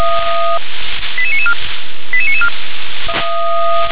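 Five-tone selective call (ZVEI), heard as quick runs of five tones, tones of about 70 ms. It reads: 88994 88994
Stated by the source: ZVEI five-tone call 88931 received on SDR.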